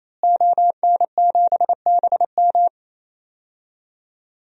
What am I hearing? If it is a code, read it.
Morse code, heard as a keyed tone, about 28 wpm, 700 Hz. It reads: ON7BM
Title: DXCC-100 Call Signs - 5 Characters - Repeated 3x - 28wpm